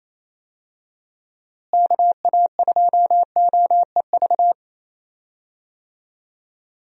Morse code, heard as a keyed tone, about 27 wpm, 700 Hz. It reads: KA2OEV
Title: USA Call Signs - 6 Characters - Repeated 2x - 28wpm